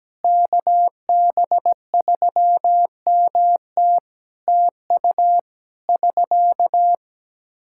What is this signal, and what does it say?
Morse code, heard as a keyed tone, about 17 wpm, 700 Hz. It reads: KB3MT TU <SK>
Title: K1USN SST QSO Practice 17wpm